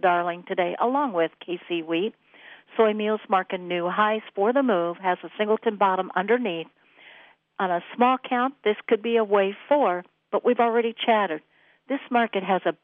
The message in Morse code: none

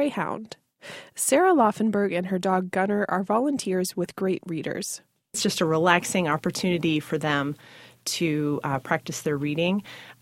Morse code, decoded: none

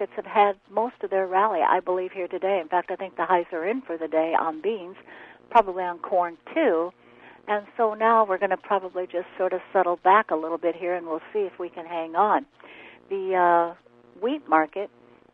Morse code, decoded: none

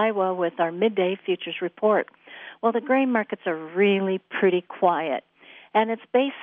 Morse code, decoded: none

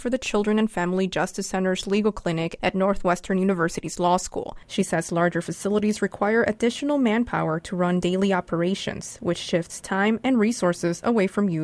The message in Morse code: none

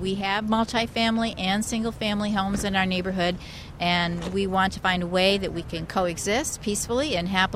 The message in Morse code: none

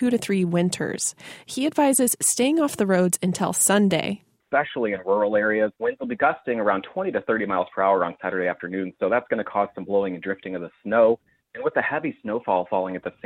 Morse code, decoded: none

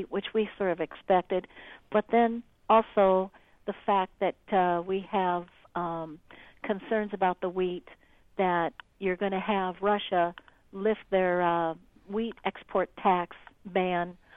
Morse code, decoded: none

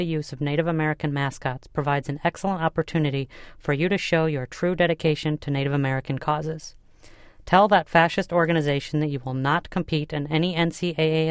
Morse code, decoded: none

